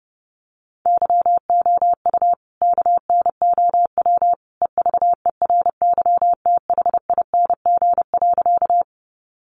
Morse code, heard as a keyed tone, about 30 wpm, 700 Hz. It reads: YOU KNOW EVERYTHING.
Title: Sentences from Top 700 Words 30wpm